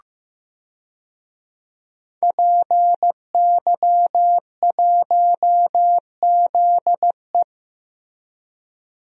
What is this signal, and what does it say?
Morse code, heard as a keyed tone, about 15 wpm, 700 Hz. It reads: PY1ZE
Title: DXCC-100 Call Signs - 5 Characters - Repeated 3x - 15wpm